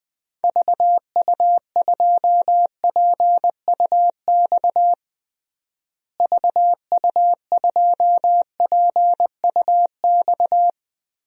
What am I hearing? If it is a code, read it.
Morse code, heard as a keyed tone, about 20 wpm, 700 Hz. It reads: VU2PUX VU2PUX
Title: DXCC-100 Call Signs - 6 Characters - Repeated 3x - 20wpm